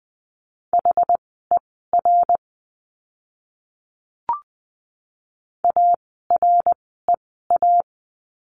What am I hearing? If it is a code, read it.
Morse code, heard as a keyed tone, about 20 wpm, spaced as for 14 wpm, 700 Hz. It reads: HER AREA
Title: 075 Lesson 13f - Letter L - Words 20wpm